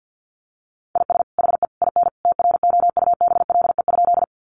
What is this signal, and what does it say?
Morse code, heard as a keyed tone, about 50 wpm, 700 Hz. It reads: IS HE IN TROUBLE?
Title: Sentences from Top 1200 Words 50wpm